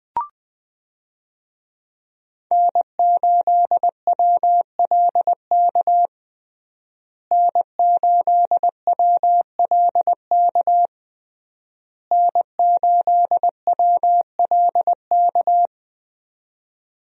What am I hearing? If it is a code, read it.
Morse code, heard as a keyed tone, about 20 wpm, 700 Hz. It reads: N8WLK N8WLK N8WLK
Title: USA Call Signs - 5 Characters - Repeated 3x - 20wpm